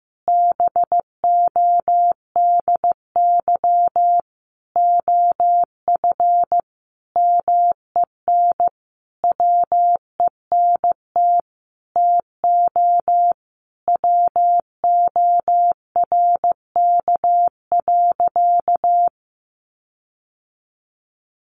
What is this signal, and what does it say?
Morse code, heard as a keyed tone, about 15 wpm, 700 Hz. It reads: BODY OF MEN WENT TO WORK.
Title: Sentences from Top 1300 Words 15wpm